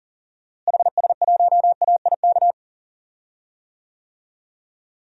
Morse code, heard as a keyed tone, about 38 wpm, 700 Hz. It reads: HS1AIK
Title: DXCC-100 Call Signs - 6 Characters - Repeated 2x - 40wpm